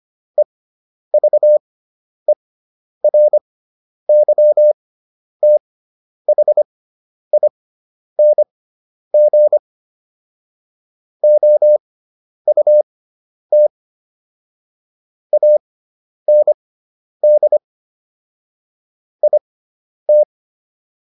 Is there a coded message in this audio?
Yes: Morse code from 0.4 to 20.2 s, heard as a keyed tone, about 25 wpm, spaced as for 10 wpm, 600 Hz. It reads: EVERYTHING OUT AND IT